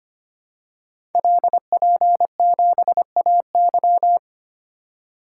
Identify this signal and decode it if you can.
Morse code, heard as a keyed tone, about 25 wpm, 700 Hz. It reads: LP7AY
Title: Code Groups - 5 Letter:Numbers - 25wpm